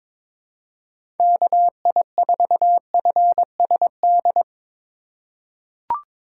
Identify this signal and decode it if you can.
Morse code, heard as a keyed tone, about 21 wpm, 700 Hz. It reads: KI4FSD
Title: USA Call Signs - 6 Characters - 3x Character Spacing - 22wpm